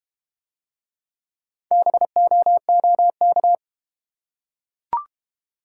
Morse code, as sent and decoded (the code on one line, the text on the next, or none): -... --- --- -.-
BOOK